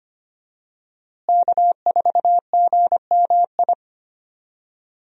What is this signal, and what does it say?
Morse code, heard as a keyed tone, about 25 wpm, 700 Hz. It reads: K4GMI
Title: USA Call Signs - 5 Characters - Repeated 2x - 25wpm